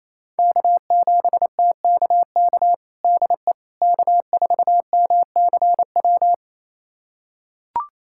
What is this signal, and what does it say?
Morse code, heard as a keyed tone, about 28 wpm, 700 Hz. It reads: K7TKK DE K4MCW
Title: USA Callsign DE Callsign - 5 Characters - 28wpm